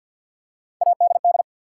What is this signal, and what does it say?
Morse code, heard as a keyed tone, about 53 wpm, 700 Hz. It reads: ADD
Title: Three Letter Words - 50wpm